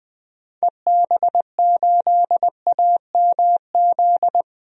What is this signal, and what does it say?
Morse code, heard as a keyed tone, about 21 wpm, 700 Hz. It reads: EB8AMZ
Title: DXCC-100 Call Signs - 6 Characters - 20wpm